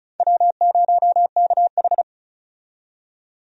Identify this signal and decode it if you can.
Morse code, heard as a keyed tone, about 34 wpm, 700 Hz. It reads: W0KH
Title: USA Call Signs - 4 Characters - 35wpm